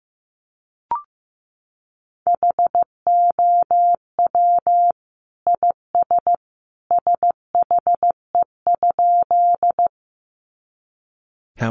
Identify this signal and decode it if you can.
Morse code, heard as a keyed tone, about 15 wpm, 700 Hz. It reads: HOW IS SHE?
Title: Sentences from Top 100 Words 15wpm